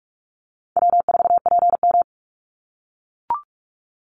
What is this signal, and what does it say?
Morse code, heard as a keyed tone, about 44 wpm, 700 Hz. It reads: W4PM
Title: Call Signs - 45wpm